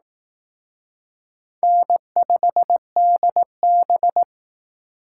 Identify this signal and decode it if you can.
Morse code, heard as a keyed tone, about 18 wpm, 700 Hz. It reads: N5DB
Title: Speed Racing - USA 1x2 Call Signs 15wpm